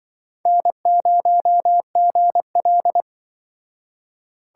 Morse code, heard as a keyed tone, about 24 wpm, 700 Hz. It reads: N0GL
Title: Speed Racing - USA 1x2 Call Signs 20wpm